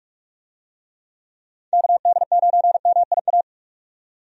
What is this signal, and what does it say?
Morse code, heard as a keyed tone, about 46 wpm, 700 Hz. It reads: KD9MIA
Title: USA Call Signs - 6 Characters - 45wpm